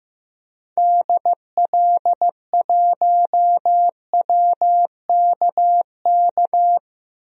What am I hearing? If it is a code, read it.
Morse code, heard as a keyed tone, about 16 wpm, 700 Hz. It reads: DL1WKK
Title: DXCC-100 Call Signs - 6 Characters - 15wpm